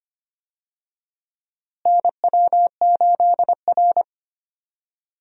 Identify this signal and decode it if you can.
Morse code, heard as a keyed tone, about 25 wpm, 700 Hz. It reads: NW8R
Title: USA Call Signs - 4 Characters - Repeated 2x - 25wpm